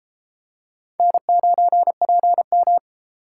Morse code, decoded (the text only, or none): N9PM